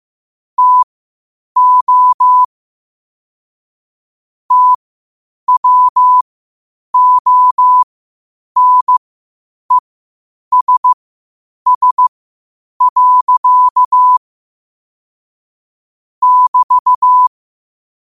Morse code, heard as a keyed tone, about 15 wpm, spaced as for 8 wpm, 1000 Hz. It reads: TO TWONESS. <BT>
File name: qotdfiles / 8wpm